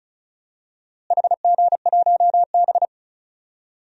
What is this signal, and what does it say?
Morse code, heard as a keyed tone, about 34 wpm, 700 Hz. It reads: HG1B
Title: DXCC-100 Call Signs - 4 Characters - Repeated 3x - 35wpm